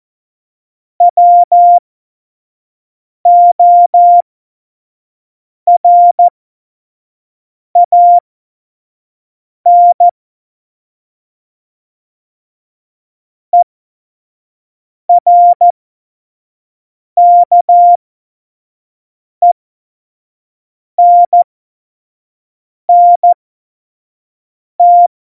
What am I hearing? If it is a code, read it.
Morse code, heard as a keyed tone, about 14 wpm, spaced as for 5 wpm, 700 Hz. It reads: WORAN ERKENNT
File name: Morsekurs / Witze / Witz 08. Januar 2019, 25 F70